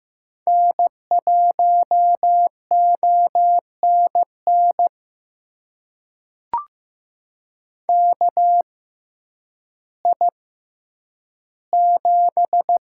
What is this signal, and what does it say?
Morse code, heard as a keyed tone, about 15 wpm, spaced as for 9 wpm, 700 Hz. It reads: N1ONN KI7
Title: USA Call Signs - 5 Characters - 6x Character Spacing - 15wpm